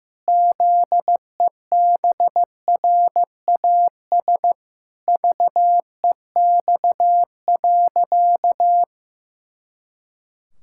Morse code, heard as a keyed tone, about 16 wpm, 700 Hz. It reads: ZEBRAS VEX.
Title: Pangram Challenge v2 - 15wpm